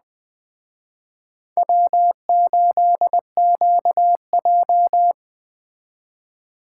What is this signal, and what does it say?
Morse code, heard as a keyed tone, about 21 wpm, 700 Hz. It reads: W8QJ
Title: Speed Racing - USA 1x2 Call Signs 17wpm